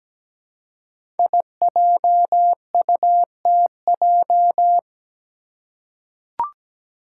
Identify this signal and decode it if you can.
Morse code, heard as a keyed tone, about 17 wpm, 700 Hz. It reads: IJUTJ
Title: Five Challenging Letter-Number Combinations 17wpm